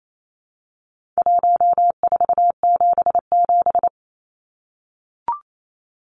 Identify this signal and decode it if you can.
Morse code, heard as a keyed tone, about 28 wpm, 700 Hz. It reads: 1477
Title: Numbers - Contest Style - 4-Digit 28wpm